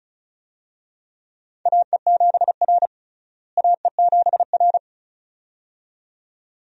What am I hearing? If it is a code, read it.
Morse code, heard as a keyed tone, about 34 wpm, 700 Hz. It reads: AE7R AE7R
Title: USA Call Signs - 4 Characters - Repeated 2x - 35wpm